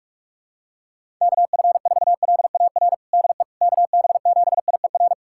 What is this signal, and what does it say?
Morse code, heard as a keyed tone, about 44 wpm, 700 Hz. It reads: KF4LAR DE KB7IER
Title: USA Callsign DE Callsign - 6 Characters - 45wpm